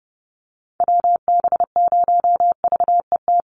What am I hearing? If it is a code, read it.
Morse code, heard as a keyed tone, about 30 wpm, 700 Hz. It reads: WB0VET
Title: USA Call Signs 30wpm